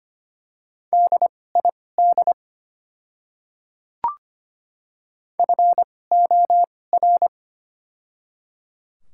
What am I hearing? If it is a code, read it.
Morse code, heard as a keyed tone, about 25 wpm, spaced as for 18 wpm, 700 Hz. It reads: DID FOR
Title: 2x Character Spacing - Top 100 Words - 25wpm